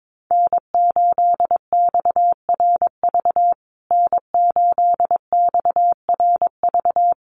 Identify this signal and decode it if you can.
Morse code, heard as a keyed tone, about 22 wpm, 700 Hz. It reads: N8XRV N8XRV
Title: USA Call Signs Repeated 3x 22wpm